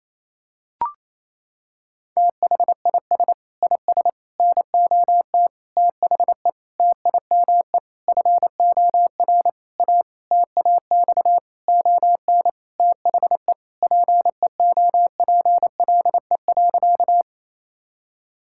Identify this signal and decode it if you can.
Morse code, heard as a keyed tone, about 28 wpm, 700 Hz. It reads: THIS IS NOT THE TIME FOR A TAX ON THE PEOPLE.
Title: Sentences from Top 1000 Words - Review Part 5 - 28wpm